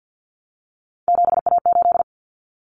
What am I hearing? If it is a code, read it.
Morse code, heard as a keyed tone, about 50 wpm, 700 Hz. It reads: 7A8
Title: Common Cut Numbers - 3-Digit 50wpm